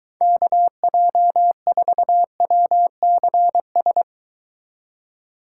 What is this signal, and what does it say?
Morse code, heard as a keyed tone, about 23 wpm, 700 Hz. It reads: KJ4WCS